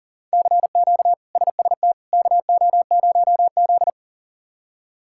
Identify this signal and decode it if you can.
Morse code, heard as a keyed tone, about 40 wpm, 700 Hz. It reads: CQ SST KO0Z
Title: K1USN SST QSO Practice - 40wpm